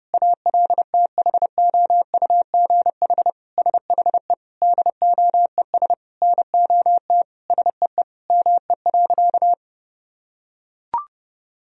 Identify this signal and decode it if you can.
Morse code, heard as a keyed tone, about 32 wpm, 700 Hz. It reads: ALTHOUGH SHE DOES NOT SEE ME.
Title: Sentences from Top 1000 Words - Review Part 3 - 30wpm